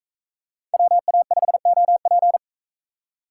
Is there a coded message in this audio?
Yes: Morse code from 0.7 to 2.4 s, heard as a keyed tone, about 42 wpm, 700 Hz. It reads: WA5OP